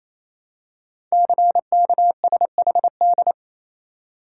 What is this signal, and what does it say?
Morse code, heard as a keyed tone, about 28 wpm, 690 Hz. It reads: CKSHD
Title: Code Groups - 5 Letters - 28wpm